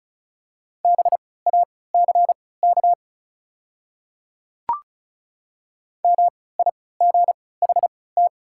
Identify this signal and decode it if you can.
Morse code, heard as a keyed tone, about 36 wpm, spaced as for 20 wpm, 690 Hz. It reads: BACK MIGHT
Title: 3x Character Spacing - Top 500 Words - 35wpm